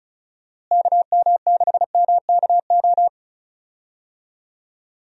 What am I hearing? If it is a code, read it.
Morse code, heard as a keyed tone, about 36 wpm, 700 Hz. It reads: KM6MKO
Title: USA Call Signs - 6 Characters - Repeated 3x - 35wpm